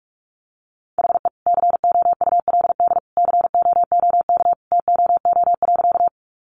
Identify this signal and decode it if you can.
Morse code, heard as a keyed tone, about 44 wpm, 700 Hz. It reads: HE COULD COOK TOO.
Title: Sentences from Top 700 Words 45wpm